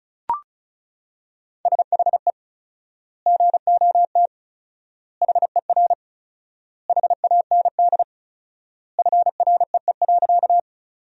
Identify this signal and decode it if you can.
Morse code, heard as a keyed tone, about 34 wpm, 700 Hz. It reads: SHE GOT HER HAND FREE.
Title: 4x Word-Spacing - Sentences from Top 500 Words - 35wpm